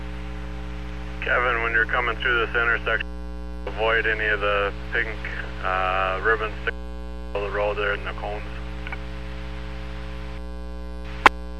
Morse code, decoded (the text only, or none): none